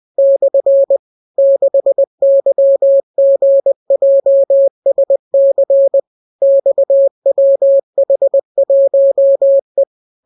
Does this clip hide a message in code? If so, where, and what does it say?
Yes: Morse code from 0.2 to 9.8 s, heard as a keyed tone, about 20 wpm, 550 Hz. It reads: / 6YGJSC XWH1E